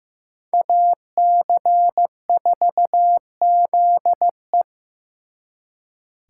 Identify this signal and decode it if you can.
Morse code, heard as a keyed tone, about 18 wpm, 700 Hz. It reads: AC4ZE
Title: Speed Racing - USA Call Signs 15wpm